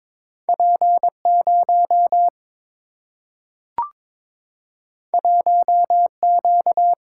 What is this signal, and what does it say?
Morse code, heard as a keyed tone, about 22 wpm, 700 Hz. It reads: P 0 1 Q
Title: Single-Letter Single-Number 22wpm